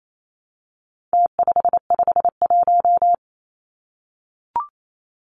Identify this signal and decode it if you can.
Morse code, heard as a keyed tone, about 28 wpm, 700 Hz. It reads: T551
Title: Numbers - Contest Style - 4-Digit 28wpm